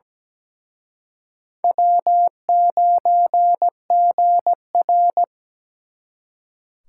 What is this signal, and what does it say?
Morse code, heard as a keyed tone, about 20 wpm, 700 Hz. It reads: W9GR
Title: Speed Racing - USA 1x2 Call Signs 17wpm